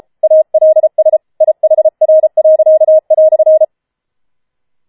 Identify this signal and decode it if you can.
Morse code, heard as a keyed tone, about 33 wpm, 630 Hz. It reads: ALS IHR."
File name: Schneewittchen Teil5 34wpm